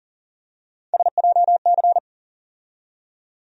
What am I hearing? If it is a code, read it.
Morse code, heard as a keyed tone, about 40 wpm, 700 Hz. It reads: SJC
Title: USA Call Sign Postfixes - 3 Letters 40wpm